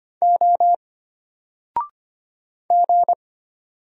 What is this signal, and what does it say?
Morse code, heard as a keyed tone, about 25 wpm, 700 Hz. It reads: O G
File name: Single-Letter-Number-25wpm